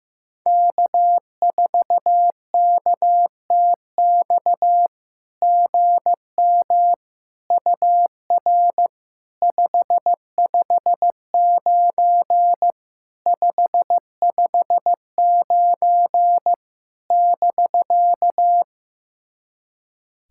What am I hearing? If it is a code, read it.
Morse code, heard as a keyed tone, about 15 wpm, 700 Hz. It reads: K4KTX GM UR 559 559 <BK>
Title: SOTA QSO Practice 15wpm